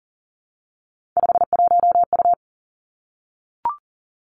Morse code, heard as a keyed tone, about 39 wpm, 700 Hz. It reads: 51U
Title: Three Challenging Letter-Number Combinations 40wpm